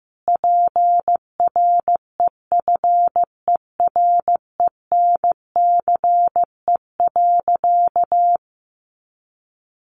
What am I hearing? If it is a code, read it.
Morse code, heard as a keyed tone, about 15 wpm, 700 Hz. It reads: PREFERENCE.
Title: Sentences from Top 1500 Words 15wpm